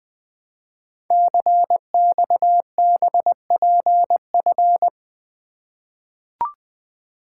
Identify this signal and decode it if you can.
Morse code, heard as a keyed tone, about 20 wpm, 700 Hz. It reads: CXBPF
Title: Code Groups - 5 Letters - 20wpm